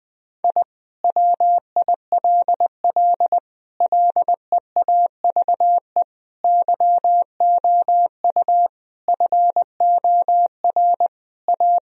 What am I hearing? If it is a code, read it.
Morse code, heard as a keyed tone, about 21 wpm, 700 Hz. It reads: I WILL LEAVE YOU FOR A
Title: Sentences from Top 1000 Words - Review Part 4 - 20wpm